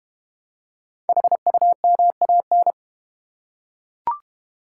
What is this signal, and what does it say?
Morse code, heard as a keyed tone, about 30 wpm, 700 Hz. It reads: HUMAN
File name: Easy Words - Speed-Racing with Hints - 25wpm